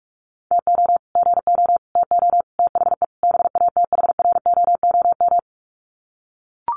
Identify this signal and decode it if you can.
Morse code, heard as a keyed tone, about 44 wpm, 700 Hz. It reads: TO GO TO THE BATHROOM
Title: Sets of 5 Words Encore 45wpm